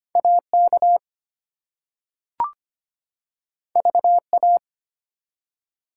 Morse code, heard as a keyed tone, about 25 wpm, 700 Hz. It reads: AK VA
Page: morse-code-ninja-practice-25wpm